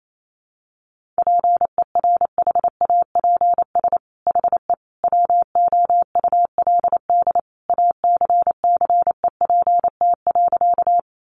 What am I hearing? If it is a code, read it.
Morse code, heard as a keyed tone, about 28 wpm, 700 Hz. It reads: PERHAPS HE WOULD ACCEPT.
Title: Sentences from Top 1300 Words 28wpm